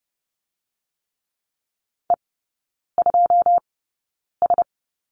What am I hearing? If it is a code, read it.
Morse code, heard as a keyed tone, about 30 wpm, spaced as for 9 wpm, 700 Hz. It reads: E2S
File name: Lesson 17e - Letter u - Three Characters 30wpm